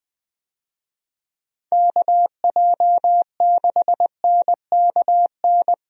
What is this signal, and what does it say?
Morse code, heard as a keyed tone, about 20 wpm, 700 Hz. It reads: KJ6NKN
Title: USA Call Signs - 6 Characters - 20wpm